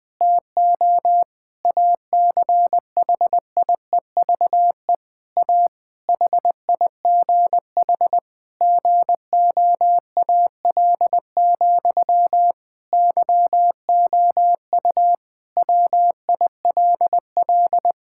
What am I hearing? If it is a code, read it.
Morse code, heard as a keyed tone, about 20 wpm, 700 Hz. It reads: TO ACHIEVE A HIGH GOAL, YOU WILL
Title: Sentences from Top 1000 Words - Review Part 6 - 20wpm